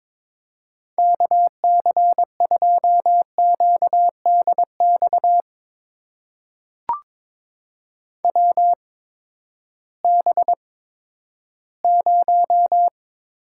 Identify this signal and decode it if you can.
Morse code, heard as a keyed tone, about 22 wpm, spaced as for 13 wpm, 700 Hz. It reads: KC2QDX WB0
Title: USA Call Signs - 6 Characters - 8x Character Spacing - 22wpm